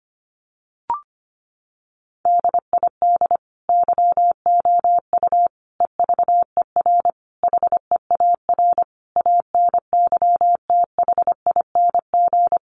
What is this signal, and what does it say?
Morse code, heard as a keyed tone, about 25 wpm, 700 Hz. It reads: DID YOU EVER HEAR ANYTHING
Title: Sentences from Top 1400 Words 25wpm